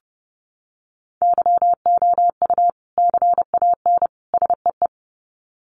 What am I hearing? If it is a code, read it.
Morse code, heard as a keyed tone, about 30 wpm, 700 Hz. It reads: YOU CAN SEE
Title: Sets of 3 Words 30wpm